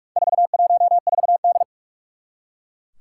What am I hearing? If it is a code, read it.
Morse code, heard as a keyed tone, about 44 wpm, 700 Hz. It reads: 414D
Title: Four Challenging Letter-Number Combinations 45wpm